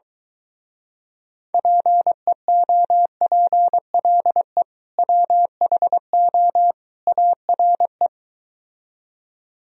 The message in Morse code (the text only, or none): PEOPLE WHO ARE